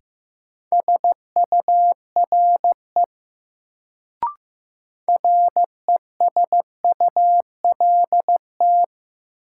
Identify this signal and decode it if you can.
Morse code, heard as a keyed tone, about 15 wpm, 700 Hz. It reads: SURE RESULT
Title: Sets of 3 Words - New Word Drill - 15wpm